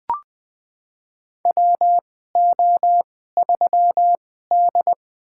Morse code, heard as a keyed tone, about 20 wpm, spaced as for 14 wpm, 700 Hz. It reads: WO3D
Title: USA Call Signs - 2x Character Spacing - 20wpm